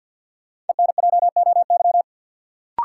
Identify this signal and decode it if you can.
Morse code, heard as a keyed tone, about 45 wpm, 700 Hz. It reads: ENJOY
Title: Sentences from Top 1000 Words - Review Part 4 - New Word Drill - 50wpm